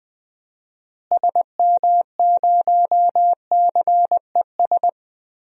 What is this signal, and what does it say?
Morse code, heard as a keyed tone, about 20 wpm, 700 Hz. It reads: SM0CES
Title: DXCC-100 Call Signs - 6 Characters - Repeated 2x - 20wpm